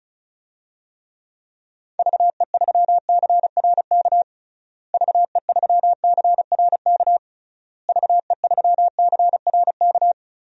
Morse code, heard as a keyed tone, about 36 wpm, 700 Hz. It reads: VE3CRK VE3CRK VE3CRK
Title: DXCC-100 Call Signs - 6 Characters - Repeated 3x - 35wpm